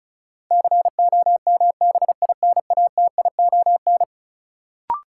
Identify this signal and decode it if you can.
Morse code, heard as a keyed tone, about 36 wpm, 700 Hz. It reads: COMBINATION
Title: Sentences from Top 1300 Words - New Word Drill Alt - 35wpm